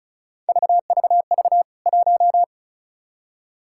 Morse code, heard as a keyed tone, about 34 wpm, 700 Hz. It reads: VVV 1
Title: K1USN SST QSO Practice 35wpm